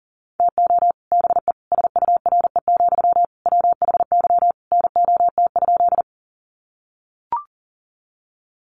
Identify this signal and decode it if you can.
Morse code, heard as a keyed tone, about 40 wpm, 700 Hz. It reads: TO BE SURE, WHY NOT?
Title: Sentences from Top 600 Words 40wpm